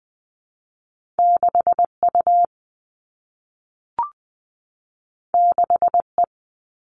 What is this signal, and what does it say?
Morse code, heard as a keyed tone, about 20 wpm, 700 Hz. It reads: 6U 6E
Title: All Cut Numbers - 2-Digit 20wpm